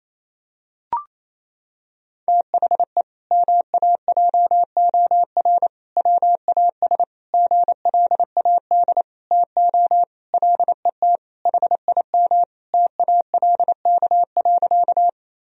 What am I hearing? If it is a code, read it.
Morse code, heard as a keyed tone, about 28 wpm, 700 Hz. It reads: THE MAJOR WAS GLAD TO LET HIM TALK.
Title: Sentences from Top 1000 Words - Review Part 4 - 28wpm